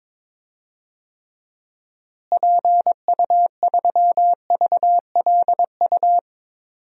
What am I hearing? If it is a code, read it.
Morse code, heard as a keyed tone, about 22 wpm, 700 Hz. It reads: PU3VLU
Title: DXCC-100 Call Signs - 6 Characters - Repeated 2x - 22wpm